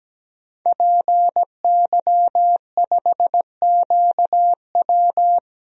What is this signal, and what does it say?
Morse code, heard as a keyed tone, about 17 wpm, 700 Hz. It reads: PY5QW
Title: DXCC-100 Call Signs - 5 Characters - Repeated 3x - 17wpm